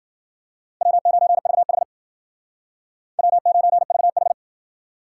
Morse code, heard as a keyed tone, about 54 wpm, 700 Hz. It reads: W9VH W9VH